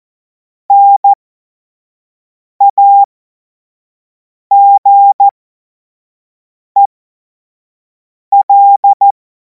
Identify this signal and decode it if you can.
Morse code, heard as a keyed tone, about 14 wpm, spaced as for 5 wpm, 800 Hz. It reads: NAGEL
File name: Morsekurs / Witze / Witz 03. Februar 2026 25 F70